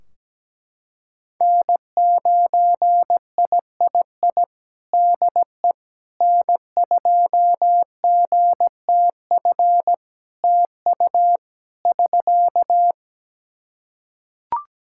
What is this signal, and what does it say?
Morse code, heard as a keyed tone, about 17 wpm, 700 Hz. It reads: N9III DE N2GTF TU <SK>
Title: Practice QSO - Exchange 4 - Ending - 17wpm